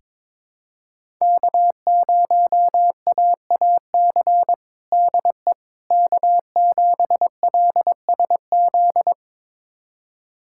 Morse code, heard as a keyed tone, about 22 wpm, 700 Hz. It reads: K0AAC DE K7LSZ
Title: USA Callsign DE Callsign - 5 Characters - 22wpm